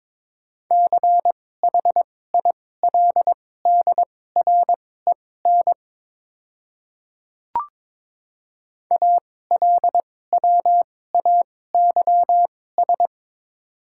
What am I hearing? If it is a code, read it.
Morse code, heard as a keyed tone, about 22 wpm, spaced as for 16 wpm, 700 Hz. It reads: CHILDREN ALWAYS
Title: 2x Character Spacing - Top 300 Words - 22wpm